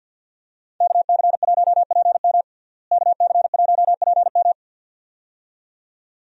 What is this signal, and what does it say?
Morse code, heard as a keyed tone, about 49 wpm, 700 Hz. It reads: KC1PM KC1PM
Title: USA Call Signs - 5 Characters - Repeated 2x - 50wpm